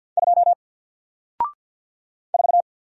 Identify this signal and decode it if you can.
Morse code, heard as a keyed tone, about 48 wpm, 700 Hz. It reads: 2 4
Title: Single Letter-Number 50wpm